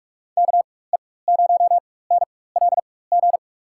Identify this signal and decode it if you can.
Morse code, heard as a keyed tone, about 44 wpm, spaced as for 21 wpm, 700 Hz. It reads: KE0NLG
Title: USA Call Signs - 4x Character Spacing - 45wpm